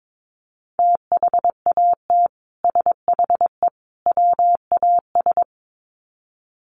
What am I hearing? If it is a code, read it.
Morse code, heard as a keyed tone, about 22 wpm, 700 Hz. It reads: THAT SHE WAS
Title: Sets of 3 Words 22wpm